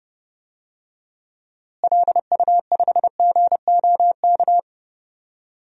Morse code, heard as a keyed tone, about 31 wpm, 700 Hz. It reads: LU5GOK
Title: DXCC-100 Call Signs - 6 Characters - Repeated 3x - 30wpm